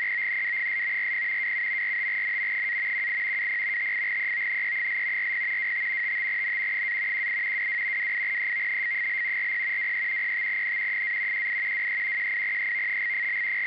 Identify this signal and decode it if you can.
Two-tone teleprinter signal, about 200 Hz shift.
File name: SITOR-B